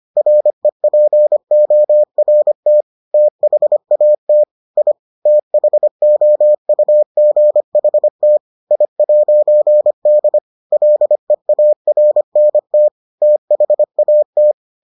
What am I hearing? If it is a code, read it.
Morse code, heard as a keyed tone, about 25 wpm, 600 Hz. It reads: REPORT THAT I THOUGHT I'D LEARNT THAT